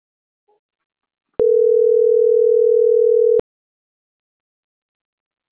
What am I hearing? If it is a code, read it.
Ringback tone (North America), 440 + 480 Hz.